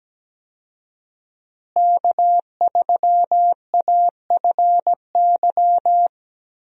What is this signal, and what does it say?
Morse code, heard as a keyed tone, about 17 wpm, 700 Hz. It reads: K3AFY
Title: USA Call Signs - 5 Characters - Repeated 2x - 17wpm